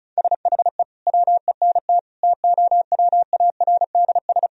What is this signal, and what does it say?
Morse code, heard as a keyed tone, about 36 wpm, 700 Hz. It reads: SHE WENT TOWARDS
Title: Sentences from Top 1000 Words - Review Part 2 - 35wpm